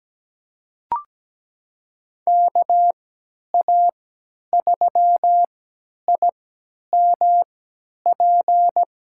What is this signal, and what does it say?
Morse code, heard as a keyed tone, about 17 wpm, spaced as for 10 wpm, 700 Hz. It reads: KA3IMP
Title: USA Call Signs - 6 Characters - 3x Character Spacing - 17wpm